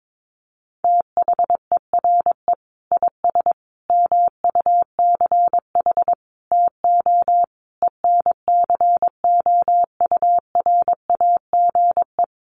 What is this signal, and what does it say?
Morse code, heard as a keyed tone, about 22 wpm, 700 Hz. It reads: THERE IS MUCH TO ENCOURAGE
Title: Sentences from Top 1400 Words 22wpm